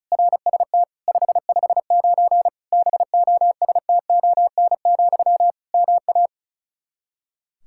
Morse code, heard as a keyed tone, about 36 wpm, 700 Hz. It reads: RST 559 BOSTON, MA
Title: Practice QSO - Exchange 1 - Beginning - 35wpm